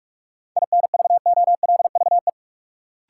Morse code, heard as a keyed tone, about 45 wpm, 700 Hz. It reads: INVOLVE
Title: Sentences from Top 1200 Words - New Word Drill - 45wpm